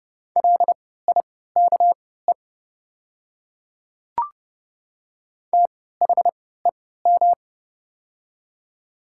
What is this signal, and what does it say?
Morse code, heard as a keyed tone, about 30 wpm, spaced as for 17 wpm, 700 Hz. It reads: LIKE THEM
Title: 3x Character Spacing - Top 100 Words - 30wpm